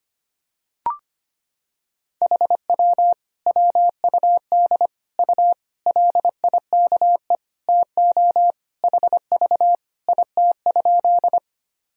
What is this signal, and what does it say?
Morse code, heard as a keyed tone, about 25 wpm, 700 Hz. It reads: HW WUD U LIKE TO HV IT?
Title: Sentences from Top 100 Words and Abbreviations 25wpm